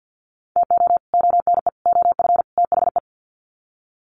Morse code, heard as a keyed tone, about 50 wpm, 700 Hz. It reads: TO ONE OF THE
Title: Sets of 4 Words 50wpm